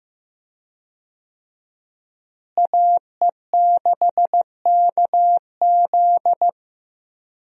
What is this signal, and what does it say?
Morse code, heard as a keyed tone, about 15 wpm, 700 Hz. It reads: AE6KZ
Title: USA Call Signs - 5 Characters - Repeated 2x - 15wpm